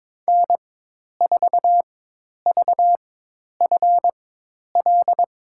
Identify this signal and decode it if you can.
Morse code, heard as a keyed tone, about 22 wpm, spaced as for 10 wpm, 700 Hz. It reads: N4VFL
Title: USA Call Signs - 5 Characters - 4x Character Spacing - 22wpm